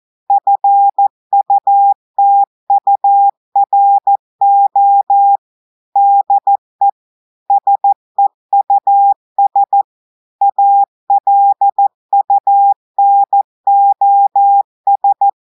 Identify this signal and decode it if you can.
Morse code, heard as a keyed tone, about 14 wpm, 800 Hz. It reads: FUTURO DE SEUS ALUNOS
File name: audios / portugais / portugais texte en clair 1 vitesse 14 mots minutes